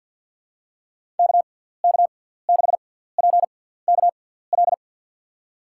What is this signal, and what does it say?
Morse code, heard as a keyed tone, about 51 wpm, spaced as for 17 wpm, 700 Hz. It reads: KK6PKL
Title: USA Call Signs - 6 Characters - 6x Character Spacing - 50wpm